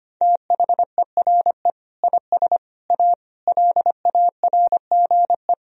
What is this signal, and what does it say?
Morse code, heard as a keyed tone, about 25 wpm, 700 Hz. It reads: THERE IS A LARGE
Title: Sentences from Top 1000 Words - Review Part 4 - 25wpm